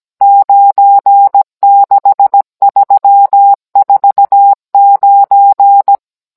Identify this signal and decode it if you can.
Morse code, heard as a keyed tone, about 17 wpm, 800 Hz. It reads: 96349